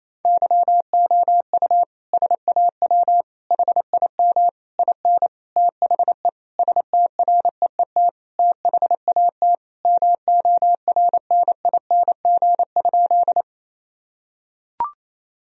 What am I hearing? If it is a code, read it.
Morse code, heard as a keyed tone, about 28 wpm, 700 Hz. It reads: YOU SAW HIM IN THE STREET THAT MORNING?
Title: Sentences from Top 1000 Words - Review Part 3 - 28wpm